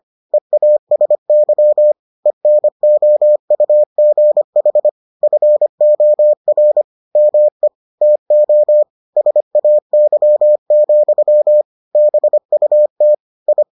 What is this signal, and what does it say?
Morse code, heard as a keyed tone, about 25 wpm, 600 Hz. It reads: EASY ENOUGH FOR ME TO SAY, BUT I